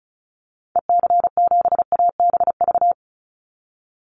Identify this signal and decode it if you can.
Morse code, heard as a keyed tone, about 34 wpm, 700 Hz. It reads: EC7ABV
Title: Call Signs - 35wpm